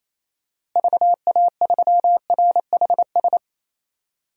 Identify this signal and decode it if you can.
Morse code, heard as a keyed tone, about 28 wpm, 700 Hz. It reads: VA3RHS
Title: DXCC-100 Call Signs - 6 Characters - Repeated 2x - 28wpm